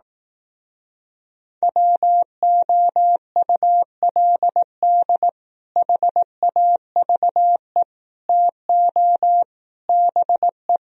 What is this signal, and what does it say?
Morse code, heard as a keyed tone, about 18 wpm, 700 Hz. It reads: WOULD HAVE TO BE
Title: Speed-Racing - Sets of 4 Words 15wpm